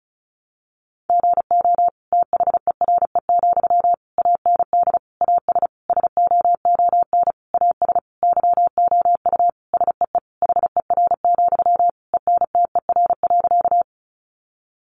Morse code, heard as a keyed tone, about 34 wpm, 700 Hz. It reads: GO THERE, AND AS SOON AS YOU SEE HER, ENTER.
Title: Sentences from Top 1300 Words 35wpm